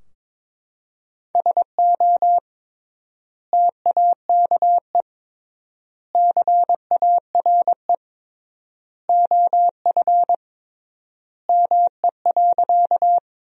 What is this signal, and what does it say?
Morse code, heard as a keyed tone, about 22 wpm, 700 Hz. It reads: SO TAKE CARE OF ME.
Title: 3x Word-Spacing - Sentences from Top 300 Words - 22wpm